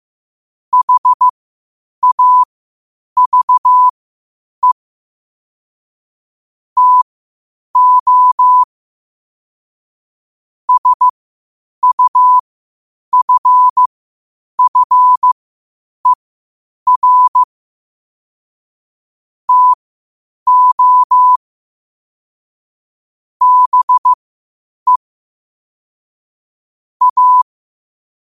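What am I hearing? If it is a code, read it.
Morse code, heard as a keyed tone, about 15 wpm, spaced as for 8 wpm, 1000 Hz. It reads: HAVE TO SUFFER TO BE A
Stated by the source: Morse practice at 8 WPM from Morse Resource.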